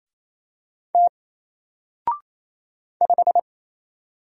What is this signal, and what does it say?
Morse code, heard as a keyed tone, about 27 wpm, 700 Hz. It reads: T 5
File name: Single-Letter-Number-28wpm